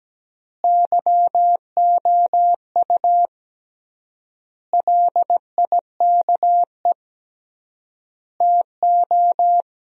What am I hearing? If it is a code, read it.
Morse code, heard as a keyed tone, about 17 wpm, 700 Hz. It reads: YOU LIKE TO
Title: 3x Word-Spacing - Sentences from Top 100 Words - 17wpm